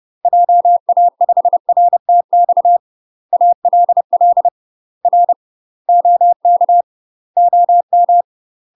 Morse code, heard as a keyed tone, about 30 wpm, 700 Hz. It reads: JA5RTX ALL R OK OM
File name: UK MASTER QSOs TRAINING / 30WPM / UK MASTER QSOs 0047